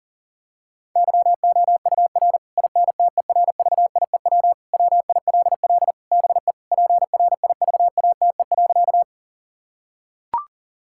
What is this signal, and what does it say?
Morse code, heard as a keyed tone, about 40 wpm, 700 Hz. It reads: YOUR INTERVIEW WILL BE PRIVATE.
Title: Sentences from Top 1600 Words 40wpm